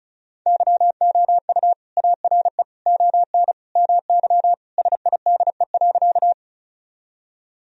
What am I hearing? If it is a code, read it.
Morse code, heard as a keyed tone, about 36 wpm, 700 Hz. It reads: YOU ARE ON MY SIDE.
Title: Sentences from Top 1000 Words - Review Part 1 - 35wpm